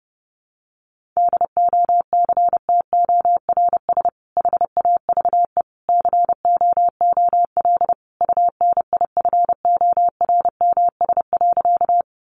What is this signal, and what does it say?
Morse code, heard as a keyed tone, about 30 wpm, 700 Hz. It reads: DOCTORS HAVE COOL UNIFORMS.
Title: Easy Phrases 30wpm